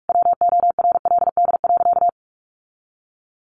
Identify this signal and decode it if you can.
Morse code, heard as a keyed tone, about 44 wpm, 700 Hz. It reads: WORLD.
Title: Sentences from Top 1400 Words 45wpm